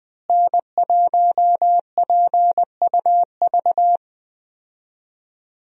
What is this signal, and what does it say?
Morse code, heard as a keyed tone, about 21 wpm, 700 Hz. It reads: N1PUV
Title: USA Call Signs - 5 Characters - Repeated 2x - 20wpm